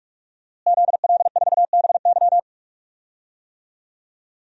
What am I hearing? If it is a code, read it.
Morse code, heard as a keyed tone, about 46 wpm, 700 Hz. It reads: ZL4BY
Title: DXCC-100 Call Signs - 5 Characters - Repeated 2x - 45wpm